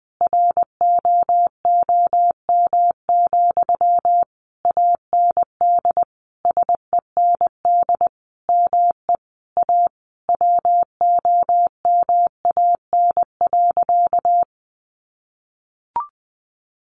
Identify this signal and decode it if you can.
Morse code, heard as a keyed tone, about 20 wpm, 700 Hz. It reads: ROOM, AND SEND ME A WOMAN.
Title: Sentences from Top 1500 Words 20wpm